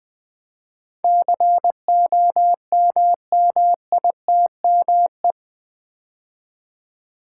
Morse code, heard as a keyed tone, about 20 wpm, 700 Hz. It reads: COMMITME
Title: Build A Word - 4 Letters At A Time - 10-Letter Words - 20wpm